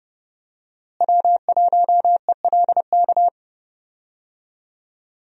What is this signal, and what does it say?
Morse code, heard as a keyed tone, about 31 wpm, 700 Hz. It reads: W1ELK
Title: USA Call Signs - 5 Characters - Repeated 3x - 30wpm